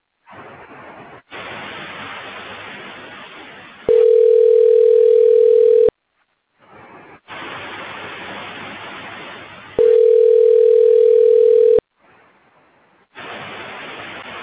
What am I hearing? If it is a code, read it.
Ringback tone (North America), 440 + 480 Hz.